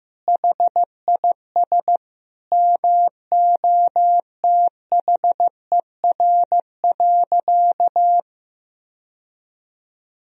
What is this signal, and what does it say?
Morse code, heard as a keyed tone, about 15 wpm, 700 Hz. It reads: HIS MOTHER.
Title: Sentences from Top 1000 Words - Review Part 4 - 15wpm